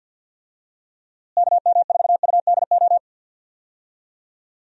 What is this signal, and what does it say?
Morse code, heard as a keyed tone, about 47 wpm, 700 Hz. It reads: KM4UDO
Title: USA Call Signs - 6 Characters - Repeated 2x - 50wpm